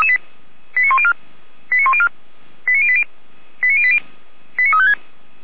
Five-tone selective call (ZVEI), heard as quick runs of five tones, tones of about 70 ms. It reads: 89104 89104 89080 89080 89356
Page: /455/SDR/ZVEI/